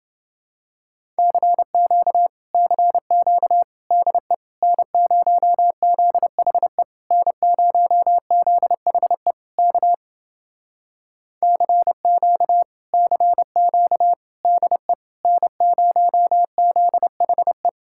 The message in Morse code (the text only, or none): CQ CQ DE N0ZHE N0ZHE K CQ CQ DE N0ZHE